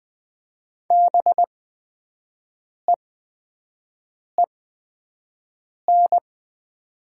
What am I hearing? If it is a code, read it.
Morse code, heard as a keyed tone, about 20 wpm, spaced as for 5 wpm, 700 Hz. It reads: BEEN
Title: 8x Character Spacing - Top 300 Words - 20wpm